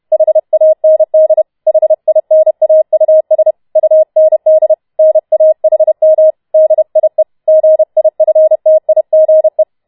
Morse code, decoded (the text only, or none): HAND HINAUS UND NAHM DIE GIFTIGE